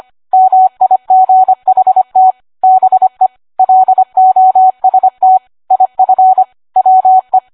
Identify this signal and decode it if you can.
Morse code, heard as a keyed tone, about 25 wpm, 750 Hz. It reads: MIGHT BE LOST IF WE